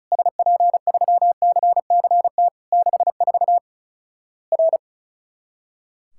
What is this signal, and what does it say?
Morse code, heard as a keyed tone, about 36 wpm, 700 Hz. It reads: SP3CCT 64 R
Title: Worked All Europe DX Contest - QTC Exchange - 35wpm